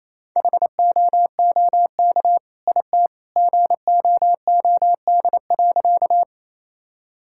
Morse code, heard as a keyed tone, about 28 wpm, 700 Hz. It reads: HOOK IT GOOD.
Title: Sentences from Top 1600 Words 28wpm